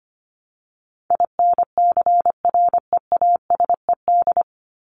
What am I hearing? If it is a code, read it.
Morse code, heard as a keyed tone, about 25 wpm, 700 Hz. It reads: INCREASED
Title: ED Suffix - Top 100 Words with Base Words 25wpm